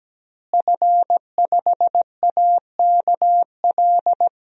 Morse code, heard as a keyed tone, about 17 wpm, 700 Hz. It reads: F5AKL
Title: DXCC-100 Call Signs - 5 Characters - Repeated 2x - 17wpm